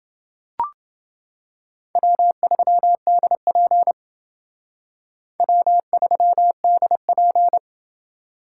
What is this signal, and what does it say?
Morse code, heard as a keyed tone, about 28 wpm, 700 Hz. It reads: W3DP W3DP